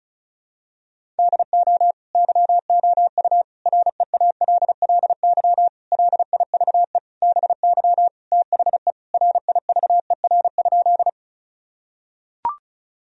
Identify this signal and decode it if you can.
Morse code, heard as a keyed tone, about 36 wpm, 700 Hz. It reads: DO YOU REALLY LIVE BY THE RIVER?
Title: Sentences from Top 1000 Words - Review Part 1 - 35wpm